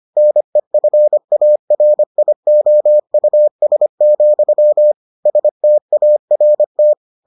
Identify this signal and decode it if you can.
Morse code, heard as a keyed tone, about 25 wpm, 600 Hz. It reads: NEFARIOUS, START